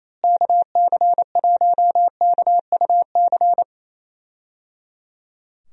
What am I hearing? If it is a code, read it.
Morse code, heard as a keyed tone, about 27 wpm, 700 Hz. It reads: KC1KUC
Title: USA Call Signs - 6 Characters - Repeated 3x - 28wpm